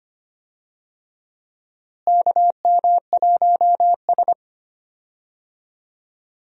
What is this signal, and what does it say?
Morse code, heard as a keyed tone, about 25 wpm, 700 Hz. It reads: KM1S
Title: USA Call Signs - 4 Characters - Repeated 3x - 25wpm